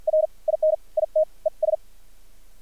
Morse code, heard as a keyed tone, about 50 wpm, spaced as for 42 wpm, 620 Hz. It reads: R IN IT ES
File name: archived / CRSnet